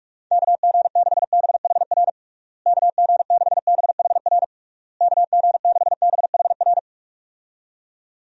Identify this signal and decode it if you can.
Morse code, heard as a keyed tone, about 46 wpm, 700 Hz. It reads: KG6BHR KG6BHR KG6BHR
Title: USA Call Signs - 6 Characters - Repeated 3x - 45wpm